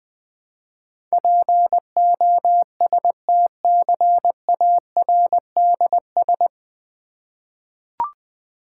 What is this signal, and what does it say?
Morse code, heard as a keyed tone, about 20 wpm, 700 Hz. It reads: POSTCARDS
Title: Difficult Words with Hints - 20wpm